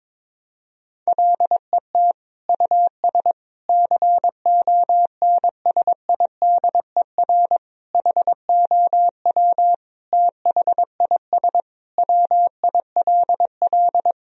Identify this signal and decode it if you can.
Morse code, heard as a keyed tone, about 22 wpm, 700 Hz. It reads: LET US CONSIDER HOW THIS WILL
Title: Sentences from Top 1000 Words - Review Part 4 - 22wpm